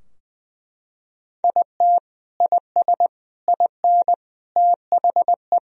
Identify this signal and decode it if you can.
Morse code, heard as a keyed tone, about 19 wpm, 700 Hz. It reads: IT IS IN THE
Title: Sentences from Top 1000 Words - Review Part 3 - 20wpm